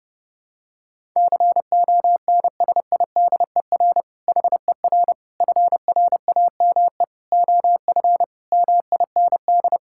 Morse code, heard as a keyed tone, about 30 wpm, 700 Hz. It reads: CONSIDER HER FRAME OF MIND
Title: Sentences from Top 1000 Words - Review Part 6 - 30wpm